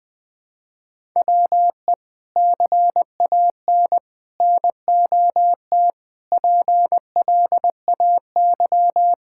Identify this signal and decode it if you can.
Morse code, heard as a keyed tone, about 20 wpm, 700 Hz. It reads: WE CAN NOT PLAY
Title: Sentences from Top 1000 Words - Review Part 6 - 20wpm